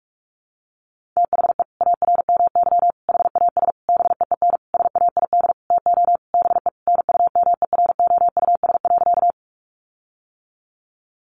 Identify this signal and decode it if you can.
Morse code, heard as a keyed tone, about 44 wpm, 700 Hz. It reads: THE ARMY HAS BEEN SAID TO BE NUMEROUS.
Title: Sentences from Top 1400 Words 45wpm